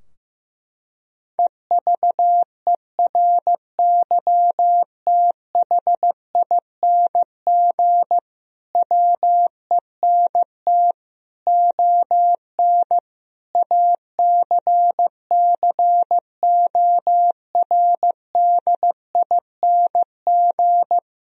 Morse code, heard as a keyed tone, about 15 wpm, 700 Hz. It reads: EVERYTHING WENT ON ACCORDING
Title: Sentences from Top 1000 Words - Review Part 4 - 15wpm